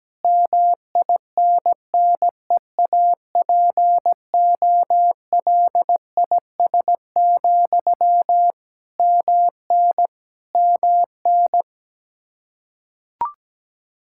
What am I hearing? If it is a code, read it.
Morse code, heard as a keyed tone, about 17 wpm, 700 Hz. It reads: MINNEAPOLIS, MN MN
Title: Practice QSO - Exchange 1 - Beginning - 17wpm